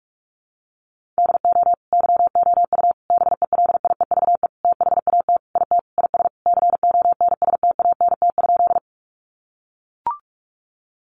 Morse code, heard as a keyed tone, about 44 wpm, 700 Hz. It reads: DO YOU BELIEVE THAT IT IS CONSTANT?
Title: Sentences from Top 1500 Words 45wpm